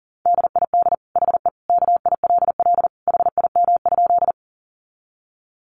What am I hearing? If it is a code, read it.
Morse code, heard as a keyed tone, about 40 wpm, 700 Hz. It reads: DID HE KILL HIM?
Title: Sentences from Top 1400 Words 40wpm